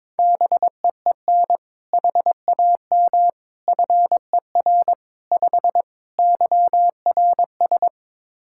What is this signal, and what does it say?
Morse code, heard as a keyed tone, about 22 wpm, 700 Hz. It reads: BEEN HAM FER 5 YRS
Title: K1USN SST QSO Practice 22wpm